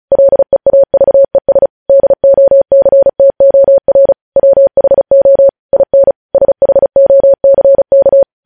Morse code, heard as a keyed tone, about 36 wpm, 550 Hz. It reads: LEAVES DOCTOR WHO IN SHOCK
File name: podcast / News headlines-1x-35x35